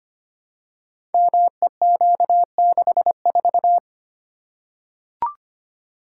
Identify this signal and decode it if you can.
Morse code, heard as a keyed tone, about 25 wpm, 700 Hz. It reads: MEQ64
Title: Code Groups - 5 Letter:Numbers - 25wpm